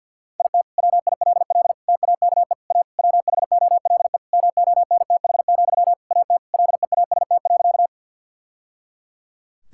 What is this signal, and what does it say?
Morse code, heard as a keyed tone, about 50 wpm, 700 Hz. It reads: IT WILL TAKE A WHOLE MONTH, AT LEAST.
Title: Sentences from Top 1000 Words - Review Part 5 - 50wpm